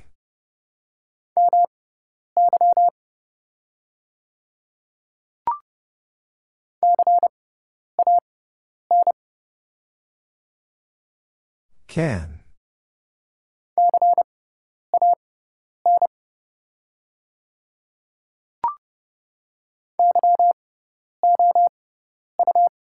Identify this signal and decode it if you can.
Morse code, heard as a keyed tone, about 30 wpm, spaced as for 10 wpm, 700 Hz. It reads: MY CAN CAN YOU
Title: 6x Character Spacing - Top 100 Words - 30wpm